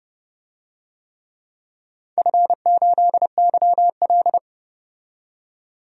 Morse code, heard as a keyed tone, about 30 wpm, 700 Hz. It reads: F8YL
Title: DXCC-100 Call Signs - 4 Characters - 30wpm